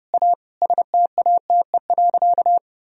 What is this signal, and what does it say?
Morse code, heard as a keyed tone, about 30 wpm, 700 Hz. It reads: A STATE.
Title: Sentences from Top 1000 Words - Review Part 2 - 30wpm